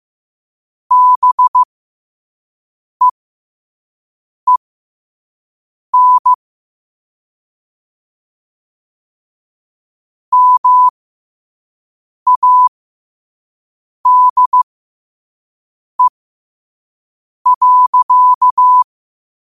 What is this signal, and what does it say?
Morse code, heard as a keyed tone, about 15 wpm, spaced as for 5 wpm, 1000 Hz. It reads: BEEN MADE.
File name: qotdfiles / 5wpm